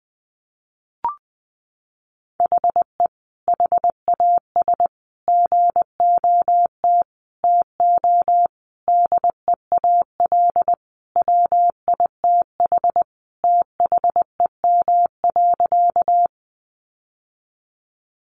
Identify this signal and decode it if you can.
Morse code, heard as a keyed tone, about 20 wpm, 700 Hz. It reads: HE HAS GOT TO DEAL WITH THEM.
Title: Sentences from Top 800 Words 20wpm